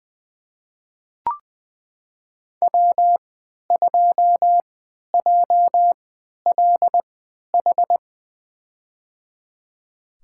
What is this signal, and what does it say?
Morse code, heard as a keyed tone, about 21 wpm, spaced as for 11 wpm, 700 Hz. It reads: W2JLH
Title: USA Call Signs - 5 Characters - 3x Character Spacing - 20wpm